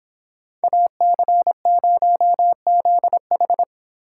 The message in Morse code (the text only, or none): AC0ZH